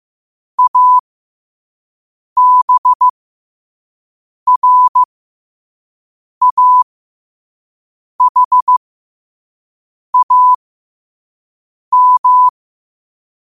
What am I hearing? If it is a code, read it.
Morse code, heard as a keyed tone, about 15 wpm, spaced as for 5 wpm, 1000 Hz. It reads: ABRAHAM
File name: qotdfiles / 5wpm